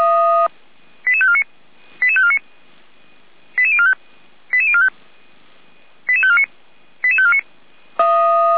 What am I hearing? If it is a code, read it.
Five-tone selective call (ZVEI), heard as quick runs of five tones, tones of about 70 ms. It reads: 88549 88549 88045 88045 88549 88549